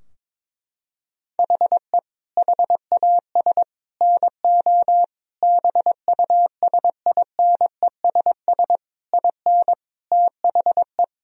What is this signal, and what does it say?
Morse code, heard as a keyed tone, about 22 wpm, 700 Hz. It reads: HE HAS NO BUSINESS IN THE
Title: Sentences from Top 1000 Words - Review Part 4 - 22wpm